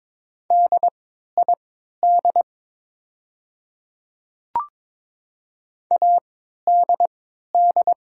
Morse code, heard as a keyed tone, about 22 wpm, spaced as for 13 wpm, 700 Hz. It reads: DID ADD
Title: 3x Character Spacing - Top 200 Words - 22wpm